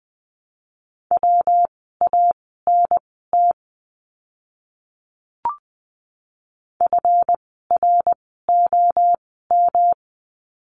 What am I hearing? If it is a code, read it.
Morse code, heard as a keyed tone, about 20 wpm, spaced as for 14 wpm, 700 Hz. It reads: WANT FROM
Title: 206 Lesson 33f - Letter j - Words 20wpm